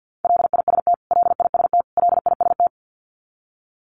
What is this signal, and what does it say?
Morse code, heard as a keyed tone, about 50 wpm, 700 Hz. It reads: LIST LIST LIST
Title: Top 500 Words Repeated 50wpm